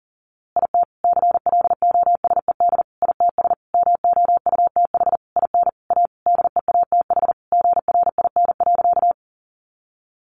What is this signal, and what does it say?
Morse code, heard as a keyed tone, about 40 wpm, 700 Hz. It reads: IT CLOSED ITS MOUTH IN A DEATH GRIN.
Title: A Sound Of Thunder 40wpm